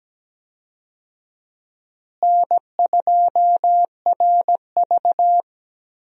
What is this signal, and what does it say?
Morse code, heard as a keyed tone, about 17 wpm, 700 Hz. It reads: N2RV
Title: USA Call Signs - 4 Characters - Repeated 3x - 17wpm